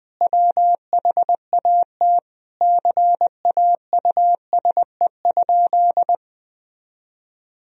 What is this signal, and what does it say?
Morse code, heard as a keyed tone, about 20 wpm, 700 Hz. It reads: WHAT CAUSE?
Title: Sentences from Top 1000 Words - Review Part 1 - 20wpm